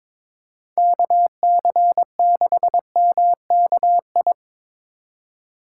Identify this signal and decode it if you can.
Morse code, heard as a keyed tone, about 22 wpm, 700 Hz. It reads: KC6MKI
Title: USA Call Signs - 6 Characters - 3x Character Spacing - 22wpm